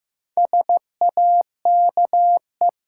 Morse code, heard as a keyed tone, about 15 wpm, 700 Hz. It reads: SAKE
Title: Sets of 4 Words - New Word Drill - 15wpm